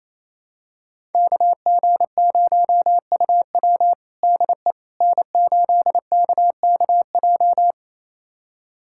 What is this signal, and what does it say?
Morse code, heard as a keyed tone, about 28 wpm, 700 Hz. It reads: KG0UW DE N8KKJ
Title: USA Callsign DE Callsign - 5 Characters - 28wpm